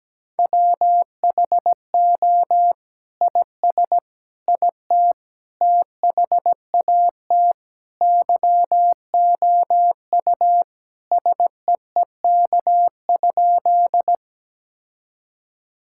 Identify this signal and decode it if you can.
Morse code, heard as a keyed tone, about 17 wpm, 700 Hz. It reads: WHO IS IT THAT YOU SEEK?
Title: Sentences from Top 1600 Words 17wpm